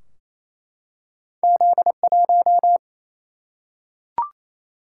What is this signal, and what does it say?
Morse code, heard as a keyed tone, about 28 wpm, 700 Hz. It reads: Z1
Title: Single-Letter Single-Number 28wpm